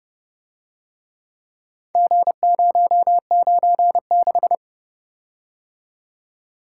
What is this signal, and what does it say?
Morse code, heard as a keyed tone, about 29 wpm, 700 Hz. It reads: G096